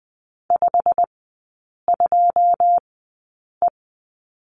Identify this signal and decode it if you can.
Morse code, heard as a keyed tone, about 20 wpm, spaced as for 8 wpm, 700 Hz. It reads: 52E
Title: 114 Lesson 19e - Letter w - Three Characters 20wpm